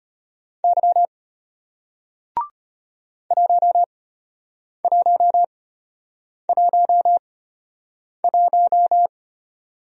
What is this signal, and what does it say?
Morse code, heard as a keyed tone, about 34 wpm, spaced as for 8 wpm, 700 Hz. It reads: Y 1111